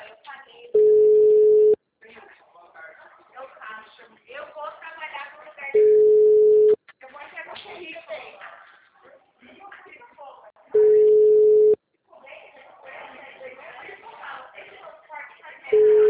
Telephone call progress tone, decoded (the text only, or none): ringback tone